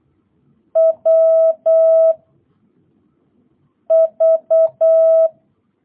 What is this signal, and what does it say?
Morse code, heard as a keyed tone, about 8 wpm, 660 Hz. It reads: W V